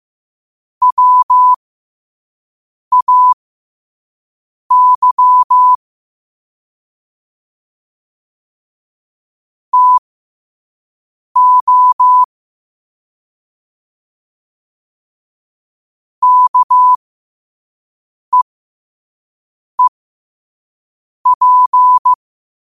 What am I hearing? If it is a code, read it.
Morse code, heard as a keyed tone, about 15 wpm, spaced as for 5 wpm, 1000 Hz. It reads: WAY TO KEEP
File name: qotdfiles / 5wpm